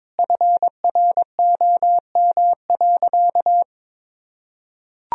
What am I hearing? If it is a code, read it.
Morse code, heard as a keyed tone, about 21 wpm, 700 Hz. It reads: FROM.
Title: Sentences from Top 1000 Words - Review Part 4 - 22wpm